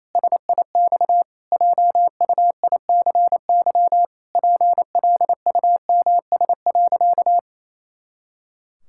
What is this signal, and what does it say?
Morse code, heard as a keyed tone, about 28 wpm, 700 Hz. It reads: SIX JUICY PLUMS.
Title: Pangram Challenge v2 - 28wpm